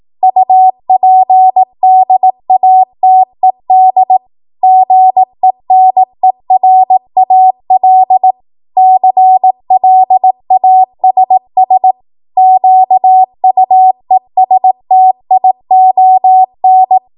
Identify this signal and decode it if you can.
Morse code, heard as a keyed tone, about 18 wpm, 750 Hz. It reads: UPDATED GENERAL CLASS QUESTION